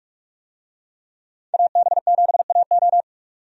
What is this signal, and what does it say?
Morse code, heard as a keyed tone, about 45 wpm, 700 Hz. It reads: AB7AO
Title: USA Call Signs - 5 Characters - 45wpm